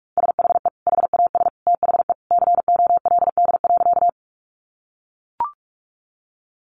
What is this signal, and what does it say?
Morse code, heard as a keyed tone, about 44 wpm, 700 Hz. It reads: SHE HAS THE COLD.
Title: Easy Phrases 45wpm